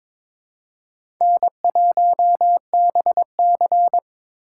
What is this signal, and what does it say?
Morse code, heard as a keyed tone, about 22 wpm, 700 Hz. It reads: N1BC
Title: USA Call Signs - 4 Characters - 2x Character Spacing - 22wpm